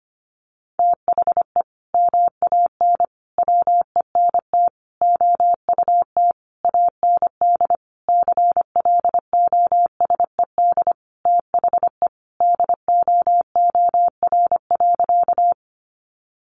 Morse code, heard as a keyed tone, about 25 wpm, 700 Hz. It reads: THE MAN WENT OUT AND CLOSED THE DOOR.
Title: Sentences from Top 1400 Words 25wpm